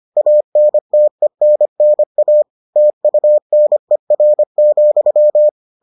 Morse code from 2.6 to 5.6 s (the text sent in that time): TUNER,